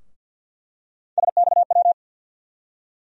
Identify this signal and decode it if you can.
Morse code, heard as a keyed tone, about 52 wpm, 700 Hz. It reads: SXW
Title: USA Call Sign Postfixes - 3 Letters 50wpm